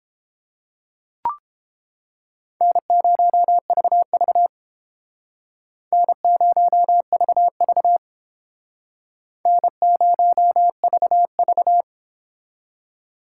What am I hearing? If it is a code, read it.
Morse code, heard as a keyed tone, about 29 wpm, 700 Hz. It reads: N0VV N0VV N0VV